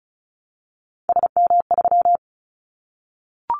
Morse code, heard as a keyed tone, about 34 wpm, 700 Hz. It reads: SM3
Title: Callsign Prefixes for Top 25 DXCC - Repeated 3x - 35wpm